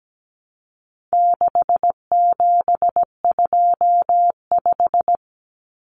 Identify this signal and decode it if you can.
Morse code, heard as a keyed tone, about 17 wpm, 700 Hz. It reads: 6725
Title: Numbers - 4-Digit 17wpm